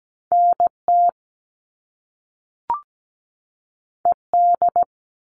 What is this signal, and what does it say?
Morse code, heard as a keyed tone, about 17 wpm, 700 Hz. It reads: NT ED
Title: All Cut Numbers - 2-Digit 17wpm